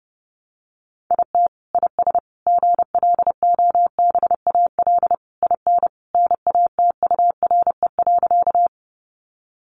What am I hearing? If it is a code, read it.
Morse code, heard as a keyed tone, about 30 wpm, 700 Hz. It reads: IT IS GLOBAL IN NATURE.
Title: Sentences from Top 1200 Words 30wpm